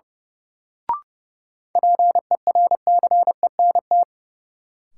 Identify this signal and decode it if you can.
Morse code, heard as a keyed tone, about 30 wpm, 700 Hz. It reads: PERCENT
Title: Sets of 2 Words - New Word Drill - 30wpm